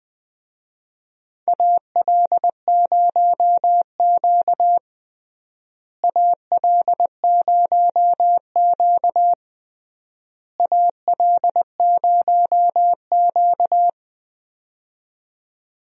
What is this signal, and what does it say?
Morse code, heard as a keyed tone, about 20 wpm, 700 Hz. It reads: AL0Q AL0Q AL0Q
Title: USA Call Signs - 4 Characters - Repeated 3x - 20wpm